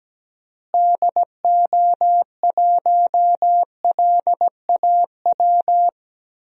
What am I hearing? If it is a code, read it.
Morse code, heard as a keyed tone, about 17 wpm, 700 Hz. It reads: DO1LAW
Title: DXCC-100 Call Signs - 6 Characters - Repeated 2x - 17wpm